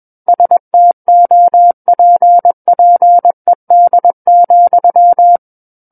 Morse code, heard as a keyed tone, about 21 wpm, 700 Hz. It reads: STOPPED,
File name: Book One / 12-Chapter 12 - 21 WPM